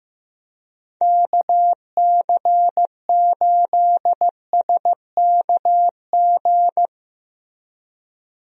Morse code, heard as a keyed tone, about 15 wpm, 700 Hz. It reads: KC8SKG
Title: USA Call Signs - 6 Characters - Repeated 2x - 15wpm